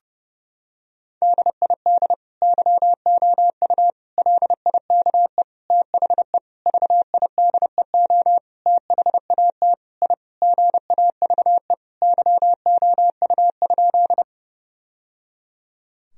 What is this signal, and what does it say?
Morse code, heard as a keyed tone, about 32 wpm, 700 Hz. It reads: DID YOU LIKE THE VIDEO THAT I GAVE YOU?
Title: Sentences from Top 1000 Words - Review Part 4 - 30wpm